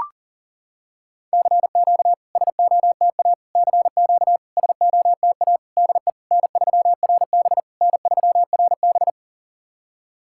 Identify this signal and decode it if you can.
Morse code, heard as a keyed tone, about 40 wpm, 700 Hz. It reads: CQ SOTA CQ SOTA DE N3RB N3RB
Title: SOTA QSO Practice 40wpm